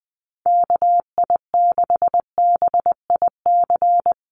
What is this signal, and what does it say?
Morse code, heard as a keyed tone, about 20 wpm, 700 Hz. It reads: KI6BIC
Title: USA Call Signs Repeated 3x 20wpm